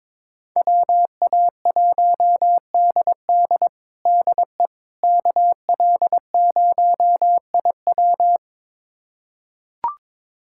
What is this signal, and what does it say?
Morse code, heard as a keyed tone, about 22 wpm, 700 Hz. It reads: WA1DD DE KL0IW
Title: USA Callsign DE Callsign - 5 Characters - 22wpm